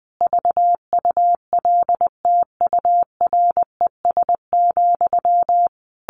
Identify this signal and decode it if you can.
Morse code, heard as a keyed tone, about 20 wpm, 700 Hz. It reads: VULTURES,
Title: A Sound Of Thunder 20wpm